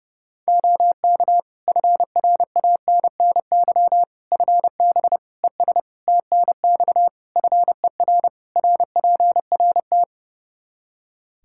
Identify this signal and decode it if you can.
Morse code, heard as a keyed tone, about 32 wpm, 700 Hz. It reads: OK FRANNY FB ES TNX FER RPRT
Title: Practice QSO - Exchange 2 - Info - 30wpm